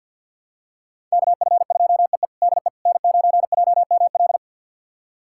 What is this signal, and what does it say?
Morse code, heard as a keyed tone, about 52 wpm, 690 Hz. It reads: KF2EE DE N9JML
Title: USA Callsign DE Callsign - 5 Characters - 50wpm